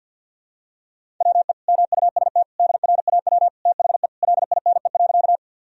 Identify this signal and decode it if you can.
Morse code, heard as a keyed tone, about 51 wpm, 700 Hz. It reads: WE MUST DRAW THE LINE.
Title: Sentences from Top 1000 Words - Review Part 1 - 50wpm